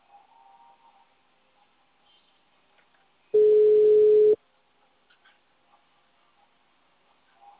Ringback tone (Europe, 425 Hz).
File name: out- -2004- - - .51